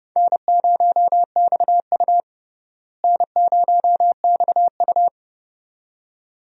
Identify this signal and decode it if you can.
Morse code, heard as a keyed tone, about 32 wpm, 700 Hz. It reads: N0XU N0XU
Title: USA Call Signs - 4 Characters - Repeated 2x - 30wpm